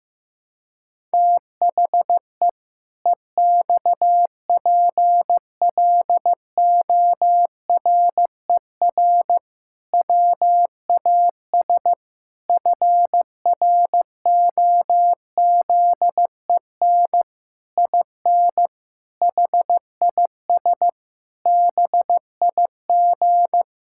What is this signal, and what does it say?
Morse code, heard as a keyed tone, about 15 wpm, 700 Hz. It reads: THE EXPLORER WAS FROZEN IN HIS BIG
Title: Pangram Challenge v2 - 15wpm